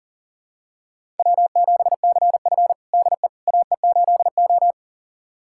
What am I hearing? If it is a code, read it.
Morse code, heard as a keyed tone, about 40 wpm, 700 Hz. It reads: W7CF DE AE8O
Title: USA Callsign DE Callsign - 4 Characters - 40wpm